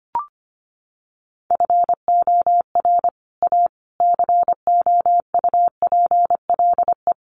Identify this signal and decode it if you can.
Morse code, heard as a keyed tone, about 25 wpm, 700 Hz. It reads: FOR A COUPLE
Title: Sets of 5 Words Encore 25wpm